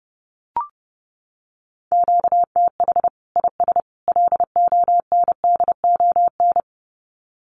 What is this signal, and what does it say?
Morse code, heard as a keyed tone, about 30 wpm, 700 Hz. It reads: QTH IS LONDON
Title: Common QSO Phrases 30wpm